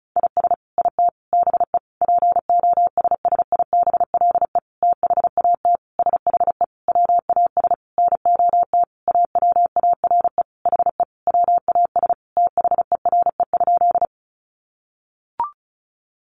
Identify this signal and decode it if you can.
Morse code, heard as a keyed tone, about 34 wpm, 700 Hz. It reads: IS IT BE POSSIBLE THAT SHE WAS NOT AWARE HE WAS THERE?
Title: Sentences from Top 1200 Words 35wpm